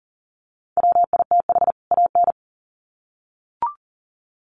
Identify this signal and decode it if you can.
Morse code, heard as a keyed tone, about 40 wpm, 700 Hz. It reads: WITH AN
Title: Sets of 2 Words 40wpm